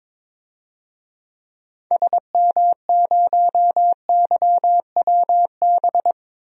Morse code, heard as a keyed tone, about 22 wpm, 700 Hz. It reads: SM0YWB
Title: DXCC-100 Call Signs - 6 Characters - 22wpm